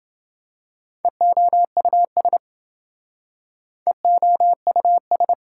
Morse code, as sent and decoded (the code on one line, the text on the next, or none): . --- ..- ... / . --- ..- ...
EOUS EOUS